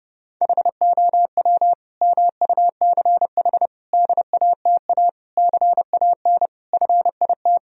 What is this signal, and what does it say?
Morse code, heard as a keyed tone, about 30 wpm, 700 Hz. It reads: HOW MUCH DATA CAN FIT
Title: Sentences from Top 1000 Words - Review Part 4 - 30wpm